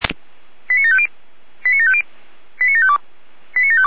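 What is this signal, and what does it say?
Five-tone selective call (ZVEI), heard as quick runs of five tones, tones of about 70 ms. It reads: 89750 89750 89752 89752
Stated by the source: ZVEI five-tone selective call received on SDR.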